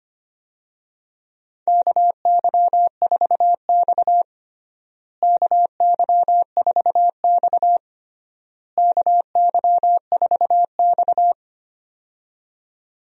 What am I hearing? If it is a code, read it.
Morse code, heard as a keyed tone, about 25 wpm, 700 Hz. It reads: KY4X KY4X KY4X
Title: USA Call Signs - 4 Characters - Repeated 3x - 25wpm